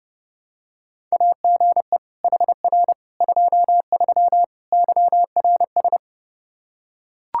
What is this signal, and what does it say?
Morse code, heard as a keyed tone, about 32 wpm, 700 Hz. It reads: AGE HR 23 YRS
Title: Practice QSO - Exchange 3 - More Info - 30wpm